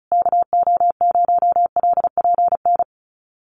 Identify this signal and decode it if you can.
Morse code, heard as a keyed tone, about 34 wpm, 700 Hz. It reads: KO0LPN
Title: USA Call Signs Repeated 3x 35wpm